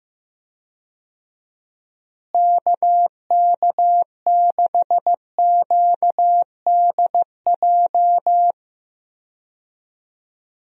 Morse code, heard as a keyed tone, about 15 wpm, 700 Hz. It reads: KK6QDJ
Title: USA Call Signs - 6 Characters - Repeated 3x - 15wpm